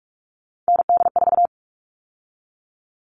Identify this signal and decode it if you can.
Morse code, heard as a keyed tone, about 44 wpm, 700 Hz. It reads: ND4
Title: All Cut Numbers - 3-Digit 45wpm